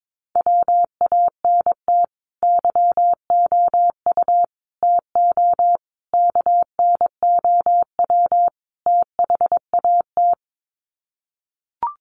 Morse code, heard as a keyed tone, about 22 wpm, 700 Hz. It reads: WANT YOU TO KNOW THAT
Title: Sets of 5 Words Encore 22wpm